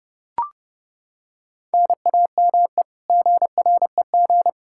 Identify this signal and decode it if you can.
Morse code, heard as a keyed tone, about 30 wpm, 700 Hz. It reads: NAME GREG
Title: Practice QSO - Exchange 1 - Beginning - 30wpm